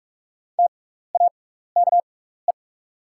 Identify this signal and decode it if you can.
Morse code, heard as a keyed tone, about 46 wpm, spaced as for 16 wpm, 700 Hz. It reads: TAKE
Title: 6x Character Spacing - Top 200 Words - 45wpm